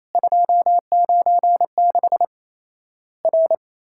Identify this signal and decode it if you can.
Morse code, heard as a keyed tone, about 28 wpm, 700 Hz. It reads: 296 R
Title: Worked All Europe DX Contest - QTC Exchange - 28wpm